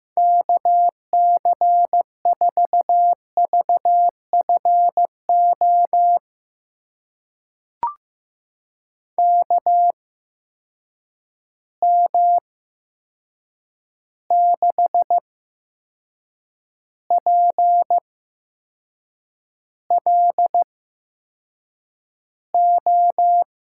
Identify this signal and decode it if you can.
Morse code, heard as a keyed tone, about 15 wpm, spaced as for 6 wpm, 700 Hz. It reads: KC4VFO KM6PLO
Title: USA Call Signs - 8x Character Spacing - 15wpm